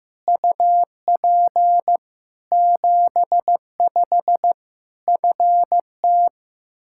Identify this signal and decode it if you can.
Morse code, heard as a keyed tone, about 15 wpm, 700 Hz. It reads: UP 75 FT
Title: K1USN SST QSO Practice 15wpm